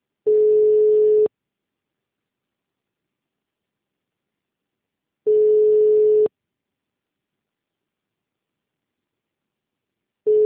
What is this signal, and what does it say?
Ringback tone (Europe, 425 Hz).
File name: out- -2016- - - .22338